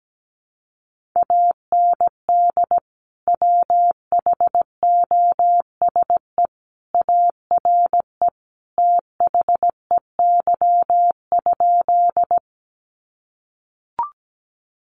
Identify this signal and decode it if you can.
Morse code, heard as a keyed tone, about 17 wpm, 700 Hz. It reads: AND WHOSE ARE THEY?
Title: Sentences from Top 1200 Words 17wpm